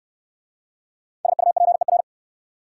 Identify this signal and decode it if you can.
Morse code, heard as a keyed tone, about 62 wpm, 700 Hz. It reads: SUPER